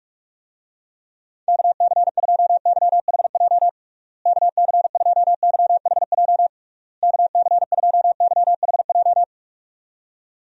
Morse code, heard as a keyed tone, about 46 wpm, 700 Hz. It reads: KC2YHJ KC2YHJ KC2YHJ
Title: USA Call Signs - 6 Characters - Repeated 3x - 45wpm